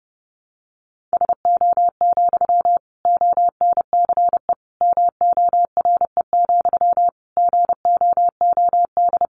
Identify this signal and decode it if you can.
Morse code, heard as a keyed tone, about 30 wpm, 700 Hz. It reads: SO, ONCE MORE, GOOD
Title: Sentences from Top 300 Words 30wpm